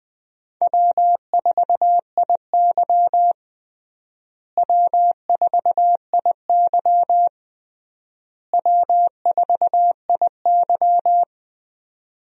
Morse code, heard as a keyed tone, about 20 wpm, 700 Hz. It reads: W4IY W4IY W4IY
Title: USA Call Signs - 4 Characters - Repeated 3x - 20wpm